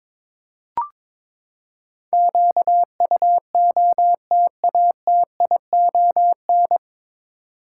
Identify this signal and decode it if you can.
Morse code, heard as a keyed tone, about 21 wpm, 700 Hz. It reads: QUOTATION
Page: Morse Code Ninja practice 22wpm